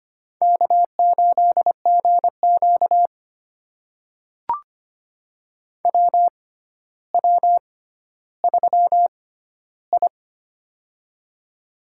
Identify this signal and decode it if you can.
Morse code, heard as a keyed tone, about 25 wpm, spaced as for 13 wpm, 700 Hz. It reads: K8GQ WW3I
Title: USA Call Signs - 4 Characters - 6x Character Spacing - 25wpm